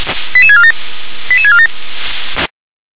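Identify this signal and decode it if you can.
Five-tone selective call (ZVEI), heard as quick runs of five tones, tones of about 70 ms. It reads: 88647 88647
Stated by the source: Received by SDR, ZVEI five-tone call 88913.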